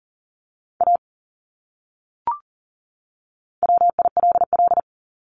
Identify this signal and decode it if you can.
Morse code, heard as a keyed tone, about 40 wpm, 700 Hz. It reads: A WILL
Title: Top 200 Words 40wpm